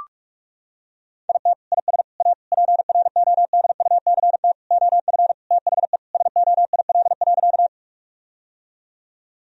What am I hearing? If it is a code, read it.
Morse code, heard as a keyed tone, about 43 wpm, 700 Hz. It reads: IT IS A PRODUCT OF THE SOIL.
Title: Sentences from Top 1000 Words - Review Part 4 - 45wpm